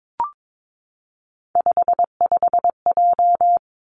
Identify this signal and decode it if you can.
Morse code, heard as a keyed tone, about 22 wpm, 700 Hz. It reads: 55J
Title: Three Challenging Letter-Number Combinations 22wpm